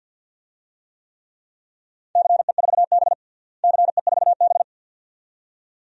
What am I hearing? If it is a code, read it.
Morse code, heard as a keyed tone, about 52 wpm, 700 Hz. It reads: CE4B CE4B
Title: DXCC-100 Call Signs - 4 Characters - Repeated 2x - 50wpm